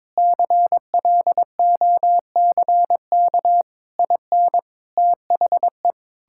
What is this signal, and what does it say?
Morse code, heard as a keyed tone, about 21 wpm, 700 Hz. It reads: CLOCK IN THE
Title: Pangram Challenge v2 - 22wpm